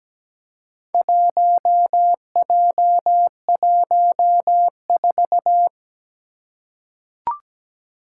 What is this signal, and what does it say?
Morse code, heard as a keyed tone, about 17 wpm, 700 Hz. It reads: 1J14
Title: Four Challenging Letter-Number Combinations 17wpm